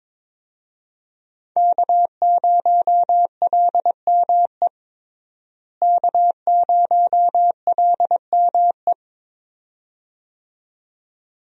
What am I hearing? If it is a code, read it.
Morse code, heard as a keyed tone, about 22 wpm, 700 Hz. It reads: K0LME K0LME
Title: USA Call Signs - 5 Characters - Repeated 2x - 22wpm